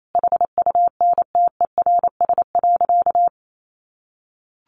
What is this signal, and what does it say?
Morse code, heard as a keyed tone, about 28 wpm, 700 Hz. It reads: HUNTERS.
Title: A Sound Of Thunder 28wpm